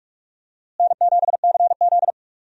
Morse code, heard as a keyed tone, about 46 wpm, 700 Hz. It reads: N7CZ
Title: USA Call Signs - 4 Characters - Repeated 3x - 45wpm